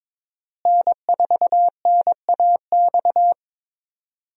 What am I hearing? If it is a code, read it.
Morse code, heard as a keyed tone, about 22 wpm, 700 Hz. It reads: N4NAX
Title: USA Call Signs - 5 Characters - Repeated 3x - 22wpm